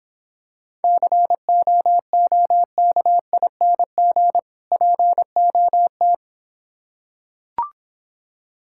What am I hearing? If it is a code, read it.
Morse code, heard as a keyed tone, about 25 wpm, 700 Hz. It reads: COOKING POT
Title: Difficult Words - Speed-Racing with Hints - 20wpm